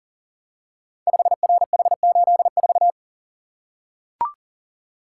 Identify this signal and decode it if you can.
Morse code, heard as a keyed tone, about 43 wpm, 700 Hz. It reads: 5RH84
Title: Code Groups - 5 Letter:Numbers - 40wpm